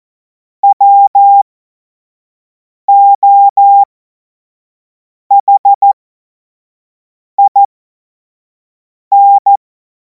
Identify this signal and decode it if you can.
Morse code, heard as a keyed tone, about 14 wpm, spaced as for 5 wpm, 800 Hz. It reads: WOHIN
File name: Morsekurs / Witze / Witz 24. März 2026 25 F70